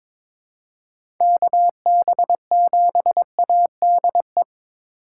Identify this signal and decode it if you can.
Morse code, heard as a keyed tone, about 22 wpm, 700 Hz. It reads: KB7ADE
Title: USA Call Signs - 4x Character Spacing - 22wpm